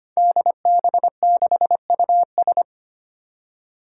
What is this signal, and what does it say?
Morse code, heard as a keyed tone, about 25 wpm, 700 Hz. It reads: DB6US
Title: Five Challenging Letter-Number Combinations 25wpm